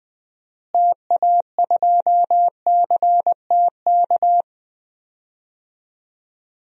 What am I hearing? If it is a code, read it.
Morse code, heard as a keyed tone, about 20 wpm, 700 Hz. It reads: TA2CTK
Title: DXCC-100 Call Signs - 6 Characters - Repeated 3x - 20wpm